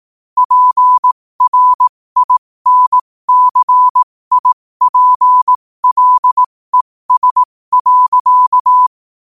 Morse code, heard as a keyed tone, about 18 wpm, spaced as for 15 wpm, 1000 Hz. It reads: PRINCIPLES.
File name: qotdfiles / 15wpm